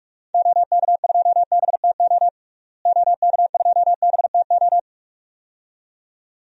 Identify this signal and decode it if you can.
Morse code, heard as a keyed tone, about 42 wpm, 700 Hz. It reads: OK2BTO OK2BTO
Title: DXCC-100 Call Signs - 6 Characters - Repeated 2x - 45wpm